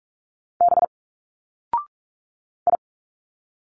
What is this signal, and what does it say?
Morse code, heard as a keyed tone, about 46 wpm, 700 Hz. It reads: B I
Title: Single Letters 45wpm